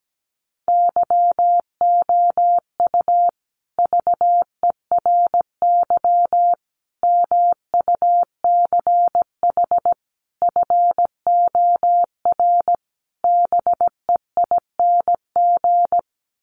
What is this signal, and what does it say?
Morse code, heard as a keyed tone, about 17 wpm, 700 Hz. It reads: YOU VERY MUCH FOR BEING
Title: Sets of 5 Words Encore 17wpm